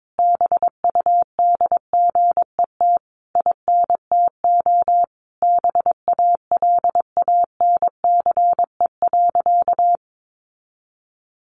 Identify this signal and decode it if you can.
Morse code, heard as a keyed tone, about 22 wpm, 700 Hz. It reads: BUDGET INTO BALANCE.
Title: Sentences from Top 1200 Words 22wpm